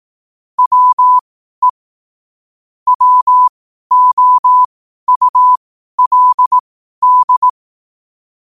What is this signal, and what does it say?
Morse code, heard as a keyed tone, about 18 wpm, spaced as for 12 wpm, 1000 Hz. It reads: WE WOULD
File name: qotdfiles / 12wpm